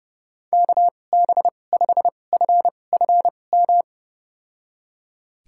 Morse code, heard as a keyed tone, about 30 wpm, spaced as for 22 wpm, 700 Hz. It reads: KB5FFM
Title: USA Call Signs - 6 Characters - 2x Character Spacing - 30wpm